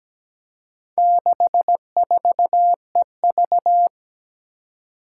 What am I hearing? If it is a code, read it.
Morse code, heard as a keyed tone, about 17 wpm, 700 Hz. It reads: 64EV
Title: Four Challenging Letter-Number Combinations 17wpm